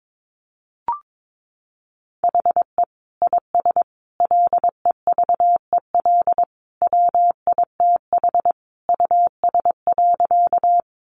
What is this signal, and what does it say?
Morse code, heard as a keyed tone, about 22 wpm, 700 Hz. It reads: HE IS LEVEL WITH US.
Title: Sentences from Top 700 Words 22wpm